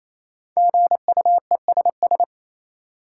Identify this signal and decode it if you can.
Morse code, heard as a keyed tone, about 28 wpm, 700 Hz. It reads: GUESS
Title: Five Letter Words - 28wpm